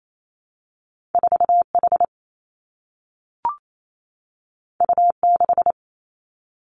Morse code, heard as a keyed tone, about 28 wpm, 700 Hz. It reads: 4 H U 6
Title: Two Challenging Letter-Number Combinations 28wpm